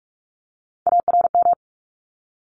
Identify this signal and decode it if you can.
Morse code, heard as a keyed tone, about 44 wpm, 700 Hz. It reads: ARM
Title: Top 1751-2000 Words 45wpm